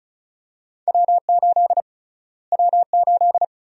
Morse code, heard as a keyed tone, about 36 wpm, 700 Hz. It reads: W8 W8
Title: USA Call Sign Prefixes - Repeated 2x - 35wpm